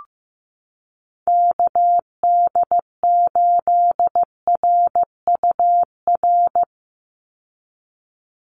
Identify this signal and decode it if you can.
Morse code, heard as a keyed tone, about 15 wpm, 700 Hz. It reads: KD8RUR
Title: USA Call Signs 15wpm